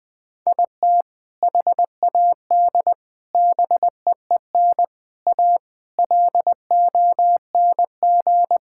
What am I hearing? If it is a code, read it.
Morse code, heard as a keyed tone, about 20 wpm, 700 Hz. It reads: IT HAD BEEN A LONG
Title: Sentences with 3-Word Phrases - Spoken Beginning - 20wpm